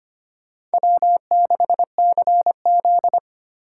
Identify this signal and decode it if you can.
Morse code, heard as a keyed tone, about 25 wpm, 700 Hz. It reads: W6CZ
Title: USA Call Signs - 4 Characters - 4x Character Spacing - 25wpm